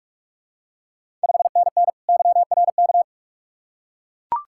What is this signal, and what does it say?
Morse code, heard as a keyed tone, about 41 wpm, 700 Hz. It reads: 5NN YRK
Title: Ontario QSO Party Exchange 45wpm